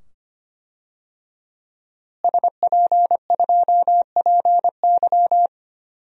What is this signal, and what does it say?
Morse code, heard as a keyed tone, about 25 wpm, 700 Hz. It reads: SP2PY
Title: DXCC-100 Call Signs - 5 Characters - Repeated 2x - 25wpm